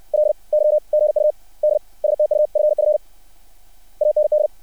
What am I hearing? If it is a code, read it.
Morse code, heard as a keyed tone, about 60 wpm, spaced as for 46 wpm, 600 Hz. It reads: LY GUD STUFF AND
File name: archived / CRSnet